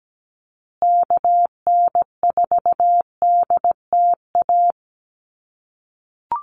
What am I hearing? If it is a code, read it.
Morse code, heard as a keyed tone, about 17 wpm, 700 Hz. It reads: KN4DTA
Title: USA Call Signs Repeated 2x 17wpm